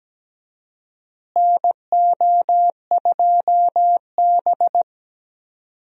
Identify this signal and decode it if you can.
Morse code, heard as a keyed tone, about 17 wpm, 700 Hz. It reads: NO2B
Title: USA Call Signs - 4 Characters - Repeated 3x - 17wpm